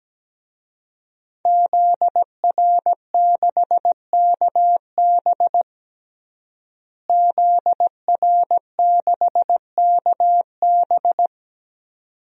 Morse code, heard as a keyed tone, about 17 wpm, 700 Hz. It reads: ZR6KB ZR6KB
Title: DXCC-100 Call Signs - 5 Characters - Repeated 2x - 17wpm